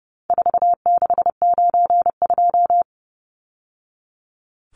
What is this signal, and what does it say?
Morse code, heard as a keyed tone, about 30 wpm, 700 Hz. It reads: 4692
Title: Numbers - 4-Digit 30wpm